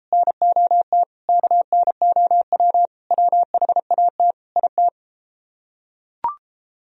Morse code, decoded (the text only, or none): NOT KNOW WHAT IT